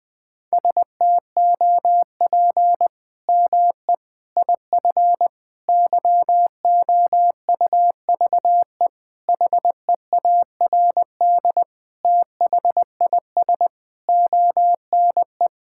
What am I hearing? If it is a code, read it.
Morse code, heard as a keyed tone, about 20 wpm, 700 Hz. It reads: STOP ME IF YOUVE HEARD THIS ONE
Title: Sets of 8 Words - 20wpm